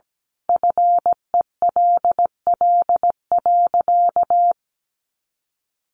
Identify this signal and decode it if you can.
Morse code, heard as a keyed tone, about 17 wpm, 700 Hz. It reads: FELL.
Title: A Sound Of Thunder 17wpm